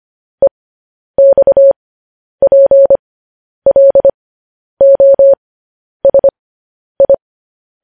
Morse code, heard as a keyed tone, about 25 wpm, spaced as for 10 wpm, 550 Hz. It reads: EXPLOSI